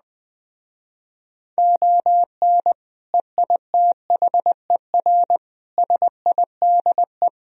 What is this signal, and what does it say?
Morse code, heard as a keyed tone, about 20 wpm, 700 Hz. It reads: ON EITHER SIDE
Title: Speed-Racing - Sets of 5 Words 17wpm